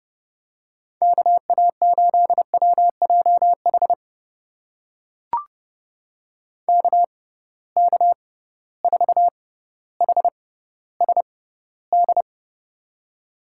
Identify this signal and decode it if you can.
Morse code, heard as a keyed tone, about 30 wpm, spaced as for 15 wpm, 700 Hz. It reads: KA8WJH KK4HSD
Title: USA Call Signs - 6 Characters - 6x Character Spacing - 30wpm